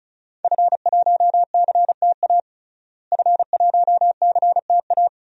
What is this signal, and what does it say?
Morse code, heard as a keyed tone, about 36 wpm, 700 Hz. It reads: F1CTA F1CTA
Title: DXCC-100 Call Signs - 5 Characters - Repeated 2x - 35wpm